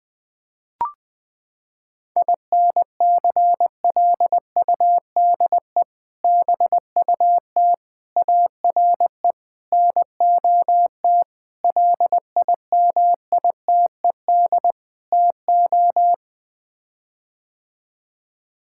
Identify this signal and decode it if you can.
Morse code, heard as a keyed tone, about 21 wpm, 700 Hz. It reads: INCLUDE BUT ARE NOT LIMITED TO
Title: Sets of 6 Words - 20wpm